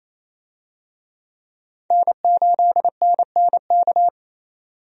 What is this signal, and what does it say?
Morse code, heard as a keyed tone, about 28 wpm, 700 Hz. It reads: N8NNK
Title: USA Call Signs - 5 Characters - 28wpm